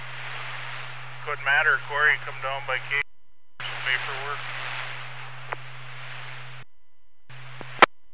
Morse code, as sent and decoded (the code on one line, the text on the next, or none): none